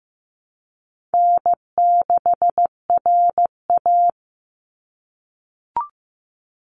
Morse code, heard as a keyed tone, about 15 wpm, 700 Hz. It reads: N6RA
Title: Easy Call Signs - 15wpm